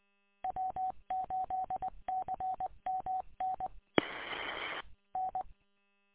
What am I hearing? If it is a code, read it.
Morse code, heard as a keyed tone, about 20 wpm, 750 Hz. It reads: W8CMN N
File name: w8cmn- -combined